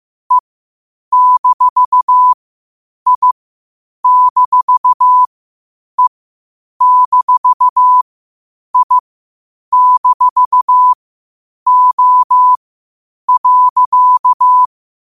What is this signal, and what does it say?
Morse code, heard as a keyed tone, about 15 wpm, spaced as for 8 wpm, 1000 Hz. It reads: E-I-E-I-O.
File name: qotdfiles / 8wpm